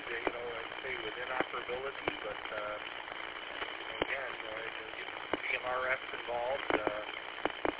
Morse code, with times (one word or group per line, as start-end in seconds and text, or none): none